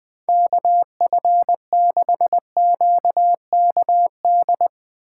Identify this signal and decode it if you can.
Morse code, heard as a keyed tone, about 21 wpm, 700 Hz. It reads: KF6QKD
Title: USA Call Signs - 6 Characters - 6x Character Spacing - 20wpm